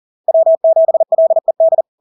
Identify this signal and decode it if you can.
Morse code, heard as a keyed tone, about 40 wpm, 650 Hz. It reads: W7LED
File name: LARK-FD-2022-Log-in-CW-40wpm-650-Hz